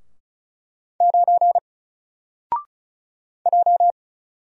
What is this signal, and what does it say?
Morse code, heard as a keyed tone, about 36 wpm, 700 Hz. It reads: 9 J
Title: Single Letter-Number 35wpm